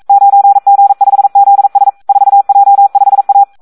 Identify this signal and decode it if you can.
Morse code, heard as a keyed tone, about 42 wpm, 800 Hz. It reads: 9G5ZS 4J5A
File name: Audio practica CW / SD 1 42